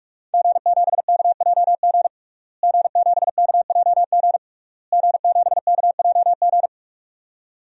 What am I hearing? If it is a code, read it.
Morse code, heard as a keyed tone, about 46 wpm, 700 Hz. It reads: G7KJG G7KJG G7KJG
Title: DXCC-100 Call Signs - 5 Characters - Repeated 3x - 45wpm